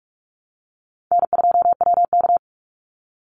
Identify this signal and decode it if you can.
Morse code, heard as a keyed tone, about 44 wpm, 700 Hz. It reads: N2WK
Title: Easy Call Signs Repeated 3x - 45wpm